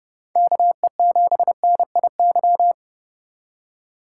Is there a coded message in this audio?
Yes: Morse code from 0.4 to 2.7 s, heard as a keyed tone, about 30 wpm, 700 Hz. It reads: KE7NIY